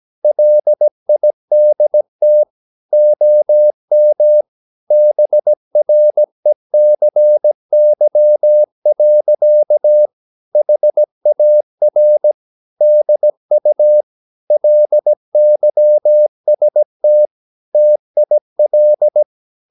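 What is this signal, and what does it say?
Morse code, heard as a keyed tone, about 17 wpm, 600 Hz. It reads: LIDT OM BRECY. HAR DU LYST TIL
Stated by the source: Slow (QRS) Morse practice at 17 wpm, from CW med Gnister.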